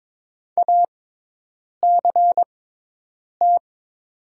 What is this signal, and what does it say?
Morse code, heard as a keyed tone, about 22 wpm, spaced as for 8 wpm, 700 Hz. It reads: ACT
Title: 6x Character Spacing - Top 200 Words - 22wpm